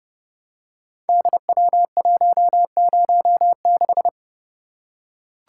Morse code, heard as a keyed tone, about 28 wpm, 700 Hz. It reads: DW106
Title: Code Groups - 5 Letter:Numbers - 30wpm